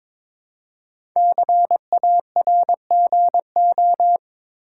Morse code, heard as a keyed tone, about 22 wpm, 700 Hz. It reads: CARGO
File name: 5 Letter Words - First 3 Letters Given - 22wpm